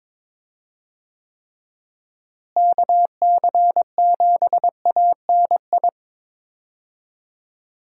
Morse code, heard as a keyed tone, about 22 wpm, 700 Hz. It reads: KC7ANI
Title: USA Call Signs - 6 Characters - Repeated 3x - 22wpm